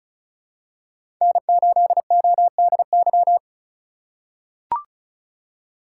Morse code, heard as a keyed tone, about 34 wpm, 700 Hz. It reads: N8ODY
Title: USA Call Signs - 5 Characters - 4x Character Spacing - 35wpm